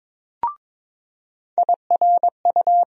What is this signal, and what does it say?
Morse code, heard as a keyed tone, about 21 wpm, 700 Hz. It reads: IRU
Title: USA Call Sign Postfixes - 3 Letters 22wpm